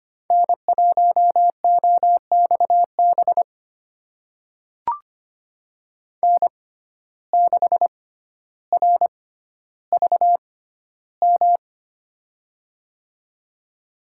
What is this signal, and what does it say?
Morse code, heard as a keyed tone, about 25 wpm, spaced as for 13 wpm, 700 Hz. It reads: N1OXB N6RVM
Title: USA Call Signs - 5 Characters - 6x Character Spacing - 25wpm